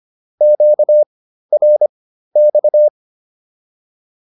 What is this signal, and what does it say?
Morse code, heard as a keyed tone, about 25 wpm, spaced as for 13 wpm, 600 Hz. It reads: QRX